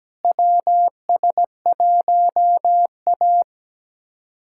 Morse code, heard as a keyed tone, about 17 wpm, 700 Hz. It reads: WS1A
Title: USA Callsign DE Callsign - 4 Characters - 17wpm